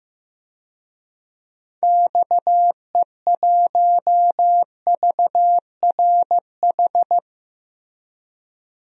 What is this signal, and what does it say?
Morse code, heard as a keyed tone, about 16 wpm, 700 Hz. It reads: XE1VRH
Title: DXCC-100 Call Signs - 6 Characters - Repeated 3x - 15wpm